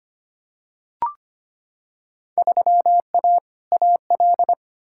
Morse code, heard as a keyed tone, about 25 wpm, 700 Hz. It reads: 3A AL
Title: Field Day Exchanges - 25wpm